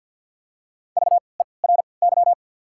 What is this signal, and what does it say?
Morse code, heard as a keyed tone, about 52 wpm, 690 Hz. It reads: V E R Y
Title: 3x Character Spacing - Top 200 Words - 50wpm